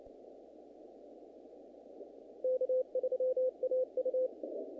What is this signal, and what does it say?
Morse code, heard as a keyed tone, about 33 wpm, 510 Hz. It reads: K3AU
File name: contests / naqp-cw-k5tr-op-k5pi- -2000